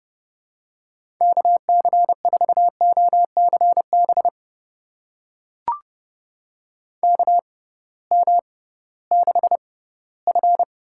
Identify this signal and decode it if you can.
Morse code, heard as a keyed tone, about 30 wpm, spaced as for 18 wpm, 700 Hz. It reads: KC4OCB KM6F
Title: USA Call Signs - 6 Characters - 6x Character Spacing - 30wpm